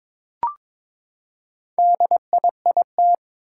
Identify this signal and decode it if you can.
Morse code, heard as a keyed tone, about 22 wpm, 700 Hz. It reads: DIIT
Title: Four Challenging Letter-Number Combinations 22wpm